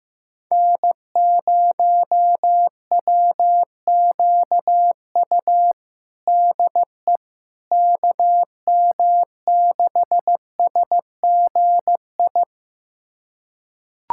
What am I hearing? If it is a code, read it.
Morse code, heard as a keyed tone, about 15 wpm, 700 Hz. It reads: N0WQU DE KM6SGI
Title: USA Callsign DE Callsign 15wpm